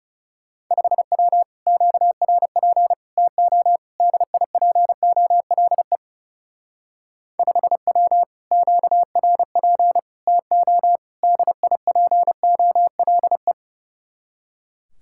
Morse code, heard as a keyed tone, about 34 wpm, 700 Hz. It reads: 5W QRP TO DIPOLE 5W QRP TO DIPOLE
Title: Speed-Racing Common QSO Phrases - 30wpm